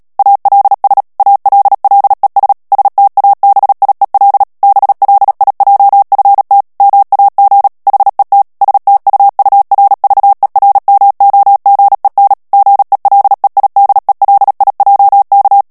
Morse code, heard as a keyed tone, about 37 wpm, 800 Hz. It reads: ALS ALLES STABIEL BLIJFT MAG HET STUURVERMOGEN GELEIDELIJK
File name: morse mp3 / HOLLANDAIS texte clair 2 vitesse 35 mots minute